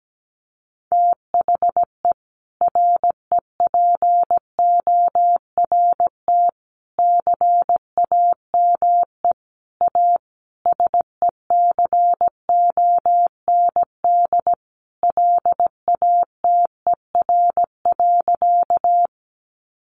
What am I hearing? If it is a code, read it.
Morse code, heard as a keyed tone, about 17 wpm, 700 Hz. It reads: THE REPORT CAME A SECOND LATER.
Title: Sentences from Top 900 Words 17wpm